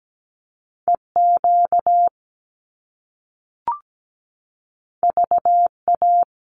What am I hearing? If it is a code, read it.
Morse code, heard as a keyed tone, about 17 wpm, 700 Hz. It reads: EQ VA
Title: Top Two Letter Combinations in Words 17wpm